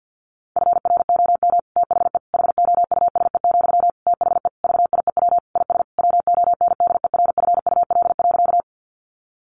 Morse code, heard as a keyed tone, about 50 wpm, 700 Hz. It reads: FROM THE HOUSE, THE VIEW IS WONDERFUL.
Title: Sentences from Top 1100 Words 50wpm